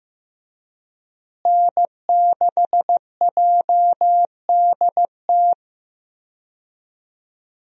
Morse code, heard as a keyed tone, about 16 wpm, 700 Hz. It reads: N6JDT
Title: USA Call Signs - 5 Characters - Repeated 3x - 15wpm